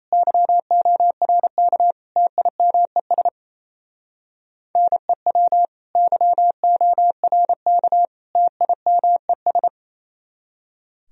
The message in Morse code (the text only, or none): YORK TIMES NEW YORK TIMES